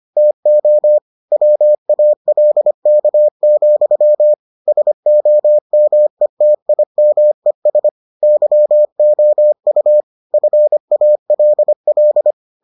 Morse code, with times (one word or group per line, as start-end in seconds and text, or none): 0.2-1.0 TO
1.3-4.3 WALK,
4.7-7.9 SOMETIMES
8.2-10.0 YOU
10.3-12.3 FALL